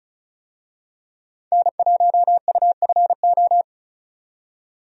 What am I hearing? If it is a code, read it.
Morse code, heard as a keyed tone, about 36 wpm, 700 Hz. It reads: N1UFO
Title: USA Call Signs - 5 Characters - 35wpm